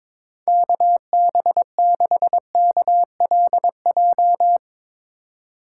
Morse code, heard as a keyed tone, about 24 wpm, 700 Hz. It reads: KB6KLJ
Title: Speed Racing - USA 2x3 Call Signs 22wpm